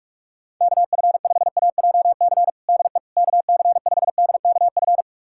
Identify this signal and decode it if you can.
Morse code, heard as a keyed tone, about 48 wpm, 700 Hz. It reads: KF5AJC DE KC5DKF
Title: USA Callsign DE Callsign - 6 Characters - 45wpm